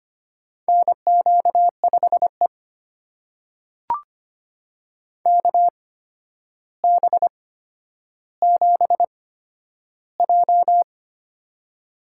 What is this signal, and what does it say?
Morse code, heard as a keyed tone, about 25 wpm, spaced as for 11 wpm, 700 Hz. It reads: NQ5E KB7J
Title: USA Call Signs - 4 Characters - 8x Character Spacing - 25wpm